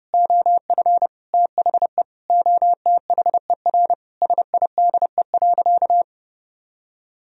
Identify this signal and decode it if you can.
Morse code, heard as a keyed tone, about 28 wpm, 700 Hz. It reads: OF THE OTHER SIDE.
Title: Sentences from Top 1000 Words - Review Part 3 - 30wpm